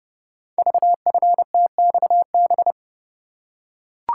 Morse code, heard as a keyed tone, about 30 wpm, 700 Hz. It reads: VFTXB
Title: Code Groups - 5 Letters - 30wpm